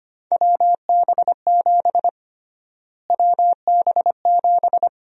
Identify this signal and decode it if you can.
Morse code, heard as a keyed tone, about 25 wpm, 700 Hz. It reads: WB7 WB7
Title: USA Call Sign Prefixes - Repeated 2x - 25wpm